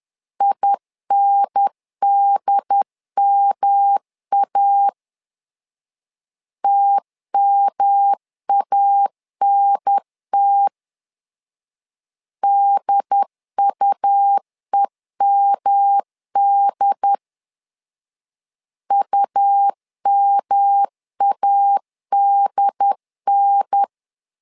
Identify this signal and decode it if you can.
Morse code, heard as a keyed tone, about 11 wpm, 800 Hz. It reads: INDMA TMANT DUEMD UMADN
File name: LEZIONI / 3 MWRO / 3 M 10wpm 1828